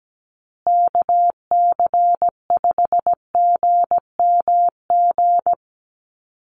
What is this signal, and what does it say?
Morse code, heard as a keyed tone, about 17 wpm, 700 Hz. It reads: KC5GMG
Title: USA Call Signs Repeated 3x 17wpm